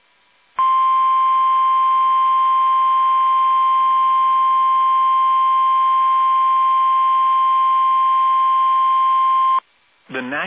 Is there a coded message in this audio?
Yes: Emergency Alert System (SAME) tones (the 1050 Hz attention tone) from 0.7 to 9.4 s.